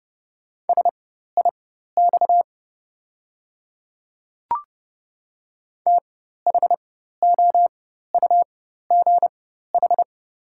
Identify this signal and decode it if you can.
Morse code, heard as a keyed tone, about 31 wpm, spaced as for 14 wpm, 700 Hz. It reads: SIX THOUGH
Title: 4x Character Spacing - Top 500 Words - 30wpm